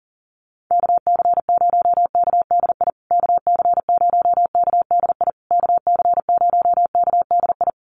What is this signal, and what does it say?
Morse code, heard as a keyed tone, about 40 wpm, 700 Hz. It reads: KC0KDI KC0KDI KC0KDI
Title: USA Call Signs Repeated 3x 40wpm